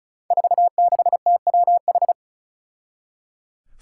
Morse code, heard as a keyed tone, about 36 wpm, 700 Hz. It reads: 46TWH
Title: Five Challenging Letter-Number Combinations 35wpm